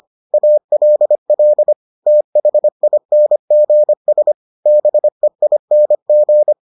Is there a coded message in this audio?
Yes: Morse code from 0.3 to 6.5 s, heard as a keyed tone, about 25 wpm, 600 Hz. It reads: ALL THINGS BEING